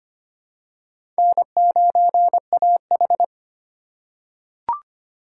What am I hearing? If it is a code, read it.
Morse code, heard as a keyed tone, about 25 wpm, 700 Hz. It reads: N9AH
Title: USA Call Signs - 3x Character Spacing - 25wpm